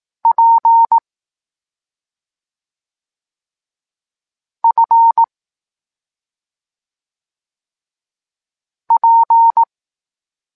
Morse code, heard as a keyed tone, about 18 wpm, 920 Hz. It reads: P F P
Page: Tekst 25 – Nauka liter P F